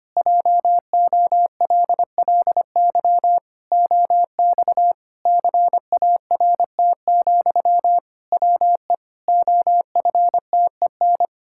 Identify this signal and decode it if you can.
Morse code, heard as a keyed tone, about 25 wpm, 700 Hz. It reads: JOLLY OX CART, WE OFTEN
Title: Pangram Challenge v2 - 25wpm